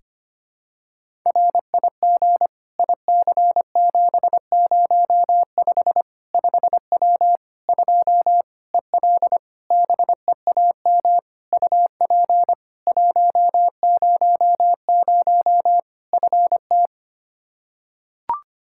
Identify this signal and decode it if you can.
Morse code, heard as a keyed tone, about 25 wpm, 700 Hz. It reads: RIG IC705 5W 2 EL BEAM UP 100 FT
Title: K1USN SST QSO Practice 25wpm